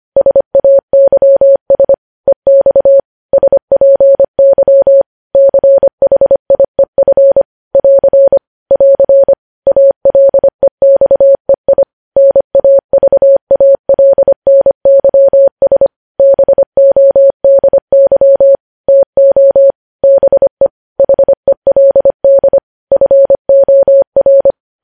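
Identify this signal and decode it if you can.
Morse code, heard as a keyed tone, about 25 wpm, 550 Hz. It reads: SAYS EX SPY CHIEF <AR> <AR> ALEXEI NAVALNYS BODY TO BE HELD FOR
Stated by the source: Morse news headlines sent at 25 words per minute.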